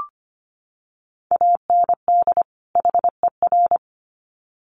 Morse code, heard as a keyed tone, about 25 wpm, 700 Hz. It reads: AND HER
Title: Sets of 2 Words 25wpm